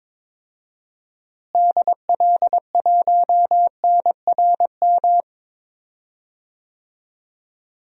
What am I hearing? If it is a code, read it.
Morse code, heard as a keyed tone, about 22 wpm, 700 Hz. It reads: DL1NRM
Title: DXCC-100 Call Signs - 6 Characters - Repeated 2x - 22wpm